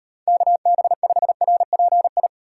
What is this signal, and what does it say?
Morse code, heard as a keyed tone, about 37 wpm, 700 Hz. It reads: KB5RPI